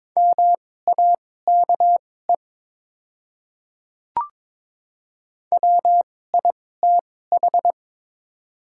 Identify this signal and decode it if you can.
Morse code, heard as a keyed tone, about 22 wpm, spaced as for 16 wpm, 700 Hz. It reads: MAKE WITH
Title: 2x Character Spacing - Top 100 Words - 22wpm